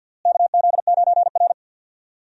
Morse code, heard as a keyed tone, about 48 wpm, 700 Hz. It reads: KZ9R
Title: USA Call Signs - 4 Characters - Repeated 3x - 50wpm